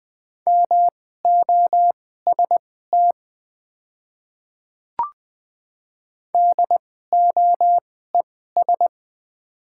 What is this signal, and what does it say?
Morse code, heard as a keyed tone, about 20 wpm, spaced as for 14 wpm, 700 Hz. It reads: MOST DOES
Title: 2x Character Spacing - Top 200 Words - 20wpm